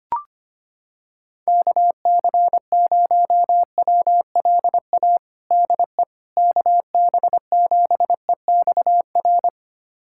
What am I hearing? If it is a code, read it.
Morse code, heard as a keyed tone, about 25 wpm, 700 Hz. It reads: KC0WLA DE KB7EXR
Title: USA Callsign DE Callsign 25wpm